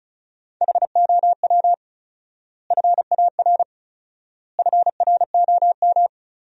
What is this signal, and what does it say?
Morse code, heard as a keyed tone, about 34 wpm, 700 Hz. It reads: HOW FAR FROM
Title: 4x Word-Spacing - Sentences from Top 400 Words - 35wpm